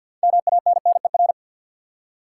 Morse code, heard as a keyed tone, about 53 wpm, 700 Hz. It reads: MANNER
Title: Six Letter Words - 50wpm